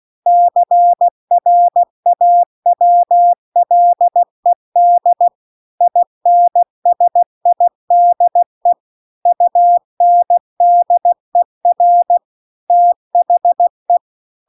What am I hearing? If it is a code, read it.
Morse code, heard as a keyed tone, about 16 wpm, 700 Hz. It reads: CRAWLED INSIDE UNDER THE